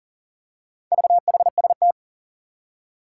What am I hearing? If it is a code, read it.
Morse code, heard as a keyed tone, about 38 wpm, 700 Hz. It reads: VHST
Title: Four Challenging Letter-Number Combinations 40wpm